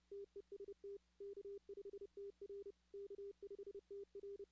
Morse code, heard as a keyed tone, about 30 wpm, 400 Hz. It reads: TEST K5TR K5TR
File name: 2016 arrl dx cw / 2016-ARRL-DX-CW-K5TR- - 0-17